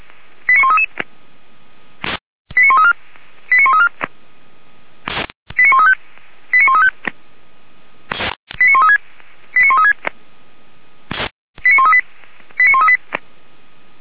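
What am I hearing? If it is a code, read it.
Five-tone selective call (ZVEI), heard as quick runs of five tones, tones of about 70 ms. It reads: 89133 89134 89134 89135 89135 89136 89136 89138 89138
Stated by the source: ZVEI five-tone selective call 89132 received via SDR, Feuerwehr Orlamünde fire brigade.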